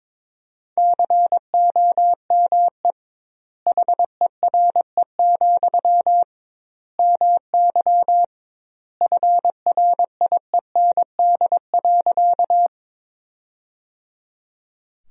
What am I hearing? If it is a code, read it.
Morse code, heard as a keyed tone, about 22 wpm, 700 Hz. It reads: COME HERE, MY FRIEND.
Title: 2x Word-Spacing - Sentences from Top 300 Words - 22wpm